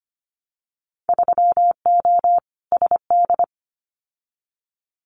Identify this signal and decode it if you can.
Morse code, heard as a keyed tone, about 25 wpm, 690 Hz. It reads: 3O SD
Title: Winter Field Day Exchanges 25wpm